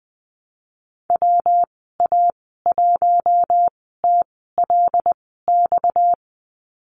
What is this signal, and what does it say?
Morse code, heard as a keyed tone, about 20 wpm, spaced as for 14 wpm, 700 Hz. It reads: WA1TLX
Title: 245 Lesson 39g - Letter z - Call Signs 20wpm